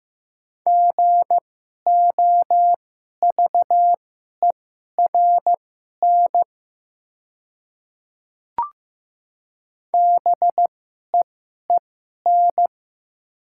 Morse code, heard as a keyed tone, about 15 wpm, spaced as for 11 wpm, 700 Hz. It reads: GOVERN BEEN
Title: 2x Character Spacing - Top 500 Words - 15wpm